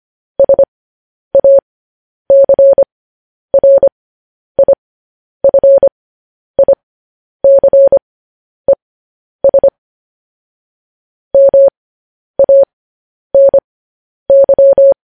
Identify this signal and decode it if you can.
Morse code, heard as a keyed tone, about 25 wpm, spaced as for 10 wpm, 550 Hz. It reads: SACRIFICES MANY
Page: news-headlines-in-morse-code-at-10-wpm